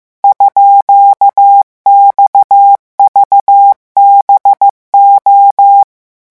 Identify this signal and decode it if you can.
Morse code, heard as a keyed tone, about 15 wpm, 800 Hz. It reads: _XVBO
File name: morse mp3 / revision12-vitesse 14 mots